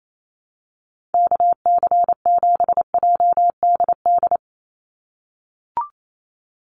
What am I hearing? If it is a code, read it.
Morse code, heard as a keyed tone, about 28 wpm, 700 Hz. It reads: KC7JDD
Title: USA Call Signs 28wpm